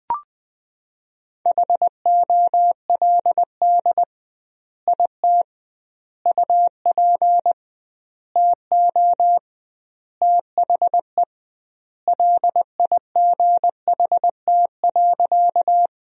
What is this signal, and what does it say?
Morse code, heard as a keyed tone, about 20 wpm, 700 Hz. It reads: HOLD IT UP TO THE LIGHT.
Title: 2x Word-Spacing - Sentences from Top 400 Words - 20wpm